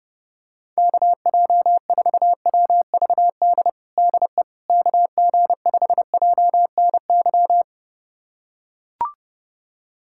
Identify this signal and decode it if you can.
Morse code, heard as a keyed tone, about 32 wpm, 700 Hz. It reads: KJ4WVD DE KG5JNY
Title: USA Callsign DE Callsign - 6 Characters - 30wpm